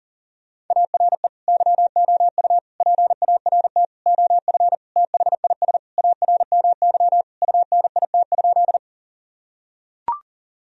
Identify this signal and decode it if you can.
Morse code, heard as a keyed tone, about 40 wpm, 700 Hz. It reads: ARE YOU PART OF THIS ARMY UNIT?
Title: Sentences from Top 1000 Words - Review Part 2 - 40wpm